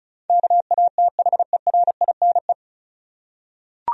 Morse code, heard as a keyed tone, about 34 wpm, 700 Hz. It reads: KATHERINE
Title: USA Common Names - 35wpm